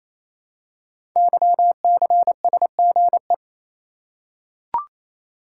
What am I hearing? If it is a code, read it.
Morse code, heard as a keyed tone, about 28 wpm, 700 Hz. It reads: YCSGE
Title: Code Groups - 5 Letters - 28wpm